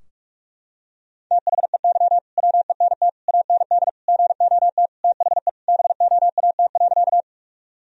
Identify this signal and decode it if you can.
Morse code, heard as a keyed tone, about 43 wpm, 700 Hz. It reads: THEY WENT AND GOT THE BOAT.
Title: Sentences from Top 1000 Words - Review Part 3 - 45wpm